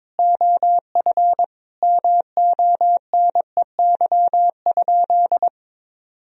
Morse code, heard as a keyed tone, about 21 wpm, 700 Hz. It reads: OF MONEY?
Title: Sentences from Top 1000 Words - Review Part 4 - 22wpm